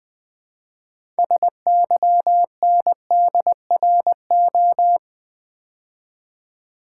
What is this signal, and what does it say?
Morse code, heard as a keyed tone, about 20 wpm, 700 Hz. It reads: SYNDRO
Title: Build A Word - 3 Letters At A Time - 8-Letter Words - 20wpm